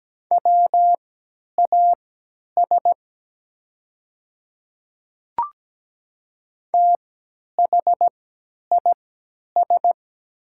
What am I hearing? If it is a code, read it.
Morse code, heard as a keyed tone, about 17 wpm, spaced as for 10 wpm, 690 Hz. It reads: WAS THIS
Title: 3x Character Spacing - Top 300 Words - 17wpm